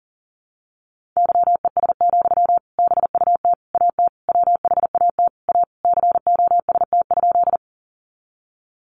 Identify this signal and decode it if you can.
Morse code, heard as a keyed tone, about 40 wpm, 700 Hz. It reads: YES, BUT AT WHAT A COST?
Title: Sentences from Top 800 Words 40wpm